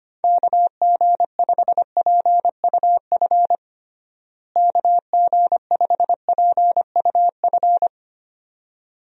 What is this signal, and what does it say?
Morse code, heard as a keyed tone, about 25 wpm, 700 Hz. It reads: KG5PUF KG5PUF
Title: USA Call Signs - 6 Characters - Repeated 2x - 25wpm